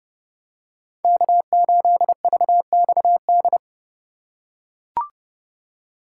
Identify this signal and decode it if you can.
Morse code, heard as a keyed tone, about 32 wpm, 700 Hz. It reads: K8VXD
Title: USA Call Signs - 5 Characters - 4x Character Spacing - 30wpm